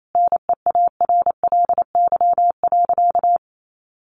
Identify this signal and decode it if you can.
Morse code, heard as a keyed tone, about 28 wpm, 700 Hz. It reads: NEARLY.
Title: Sentences from Top 1300 Words 28wpm